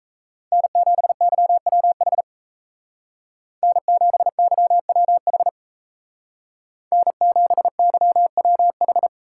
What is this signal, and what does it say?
Morse code, heard as a keyed tone, about 36 wpm, 700 Hz. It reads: N7YWH N7YWH N7YWH